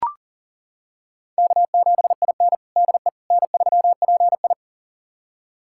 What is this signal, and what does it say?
Morse code, heard as a keyed tone, about 40 wpm, 700 Hz. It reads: K7IN DE N3PI
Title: USA Callsign DE Callsign - 4 Characters - 40wpm